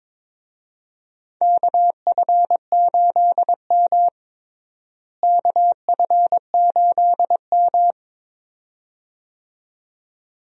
Morse code, heard as a keyed tone, about 22 wpm, 700 Hz. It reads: KF8M KF8M
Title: USA Call Signs - 4 Characters - Repeated 2x - 22wpm